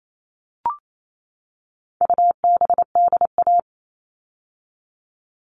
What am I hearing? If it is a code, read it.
Morse code, heard as a keyed tone, about 28 wpm, 700 Hz. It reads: UBDA
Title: All Cut Numbers - 4-Digit 28wpm